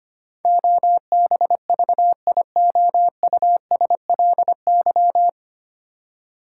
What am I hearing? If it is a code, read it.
Morse code, heard as a keyed tone, about 25 wpm, 700 Hz. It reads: OBVIOUSLY
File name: Nine Letter Words - 25wpm